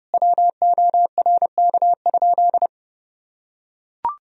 Morse code, heard as a keyed tone, about 30 wpm, 700 Hz. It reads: WORK?
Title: Sentences from Top 500 Words and Abbreviations 30wpm